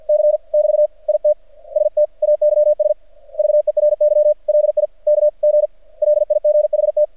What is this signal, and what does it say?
Morse code, heard as a keyed tone, about 51 wpm, 600 Hz. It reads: C <BT> IT STAYS VERY LIKG LIGHT
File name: archived / CRSnet